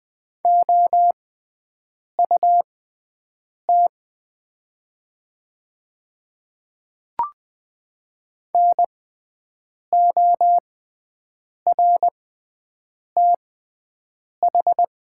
Morse code, heard as a keyed tone, about 20 wpm, spaced as for 7 wpm, 700 Hz. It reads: OUT NORTH
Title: 6x Character Spacing - Top 300 Words - 20wpm